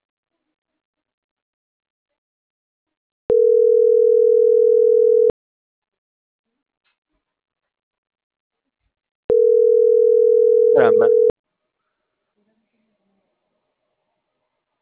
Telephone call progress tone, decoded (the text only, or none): ringback tone